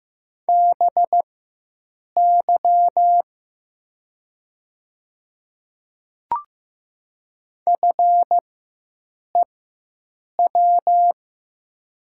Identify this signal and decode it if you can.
Morse code, heard as a keyed tone, about 15 wpm, spaced as for 7 wpm, 700 Hz. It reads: BY FEW
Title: 4x Character Spacing - Top 300 Words - 15wpm